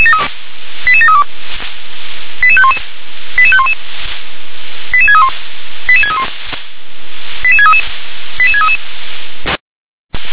Five-tone selective call (ZVEI), heard as quick runs of five tones, tones of about 70 ms. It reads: 88732 88511 88511 88631 88631 88633 88633
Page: SDR/ZVEI